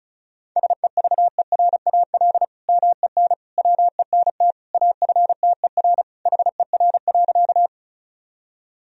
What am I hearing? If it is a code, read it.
Morse code, heard as a keyed tone, about 34 wpm, 700 Hz. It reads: SEVERAL MEN WENT AFTER HER.
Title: Sentences from Top 1000 Words - Review Part 2 - 35wpm